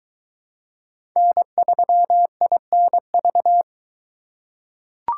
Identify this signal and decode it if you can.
Morse code, heard as a keyed tone, about 22 wpm, 700 Hz. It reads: N3INV